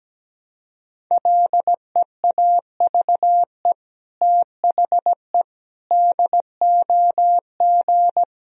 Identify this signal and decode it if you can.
Morse code, heard as a keyed tone, about 17 wpm, 700 Hz. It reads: LEAVE THE DOG
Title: Sentences from Top 1000 Words - Review Part 2 - 17wpm